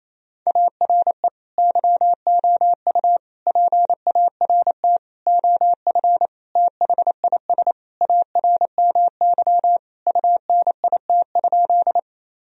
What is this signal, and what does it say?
Morse code, heard as a keyed tone, about 28 wpm, 700 Hz. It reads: ARE YOU PART OF THIS ARMY UNIT?
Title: Sentences from Top 1000 Words - Review Part 2 - 28wpm